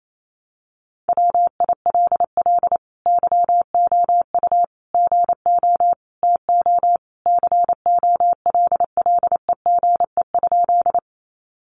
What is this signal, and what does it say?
Morse code, heard as a keyed tone, about 28 wpm, 700 Hz. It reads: WILL YOU GO TO COLLEGE?
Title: Sentences from Top 1300 Words 28wpm